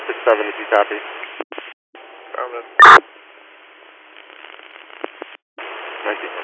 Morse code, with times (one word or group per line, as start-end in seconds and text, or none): none